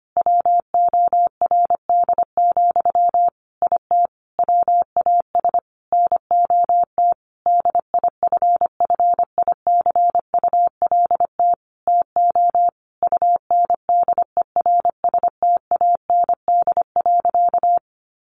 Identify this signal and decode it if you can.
Morse code, heard as a keyed tone, about 25 wpm, 700 Hz. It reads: WORD, IT WAS NOT DIFFICULT TO UNDERSTAND.
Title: Sentences from Top 1100 Words 25wpm